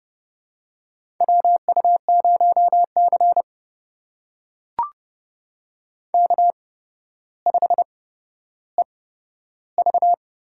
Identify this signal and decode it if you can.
Morse code, heard as a keyed tone, about 29 wpm, spaced as for 13 wpm, 700 Hz. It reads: WU0C K5EV
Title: USA Call Signs - 4 Characters - 8x Character Spacing - 30wpm